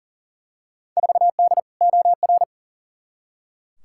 Morse code, heard as a keyed tone, about 40 wpm, 700 Hz. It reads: 4D OR
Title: Field Day Exchanges Encore - 40wpm